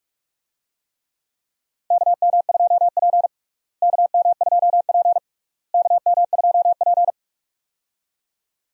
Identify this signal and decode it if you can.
Morse code, heard as a keyed tone, about 46 wpm, 700 Hz. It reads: KM2P KM2P KM2P
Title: USA Call Signs - 4 Characters - Repeated 3x - 45wpm